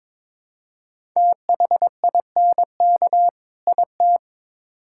Morse code, heard as a keyed tone, about 21 wpm, 700 Hz. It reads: THINK IT
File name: Build A Phrase - 2-Word Phrases - 22wpm